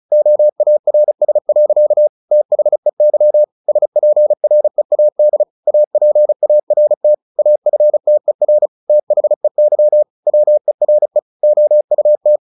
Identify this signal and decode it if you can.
Morse code, heard as a keyed tone, about 35 wpm, 600 Hz. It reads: OARS. THEY SPREAD APART AFTER THEY WERE OUT
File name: the old man and the sea / the old man and the sea 1 35wpm